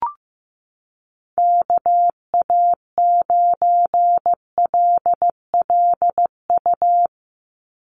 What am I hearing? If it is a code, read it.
Morse code, heard as a keyed tone, about 15 wpm, 700 Hz. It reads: KA9LLU
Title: USA Call Signs 15wpm